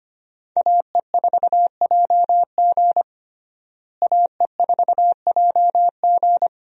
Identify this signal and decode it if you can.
Morse code, heard as a keyed tone, about 25 wpm, 700 Hz. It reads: AE4JG AE4JG
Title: USA Call Signs - 5 Characters - Repeated 3x - 25wpm